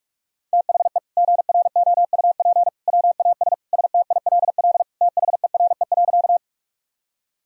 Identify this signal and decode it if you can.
Morse code, heard as a keyed tone, about 46 wpm, 700 Hz. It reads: THE GROUP WAS STILL THERE.
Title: Sentences from Top 1000 Words - Review Part 1 - 45wpm